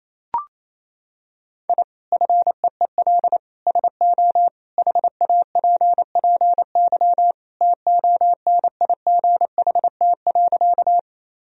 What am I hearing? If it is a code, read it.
Morse code, heard as a keyed tone, about 28 wpm, 700 Hz. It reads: I FEEL SO HAPPY TONIGHT.
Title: Sentences from Top 1000 Words - Review Part 3 - 28wpm